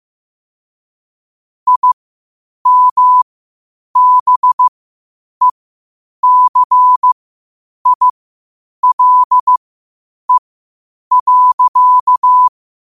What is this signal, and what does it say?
Morse code, heard as a keyed tone, about 15 wpm, spaced as for 8 wpm, 1000 Hz. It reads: IMBECILE.
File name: qotdfiles / 8wpm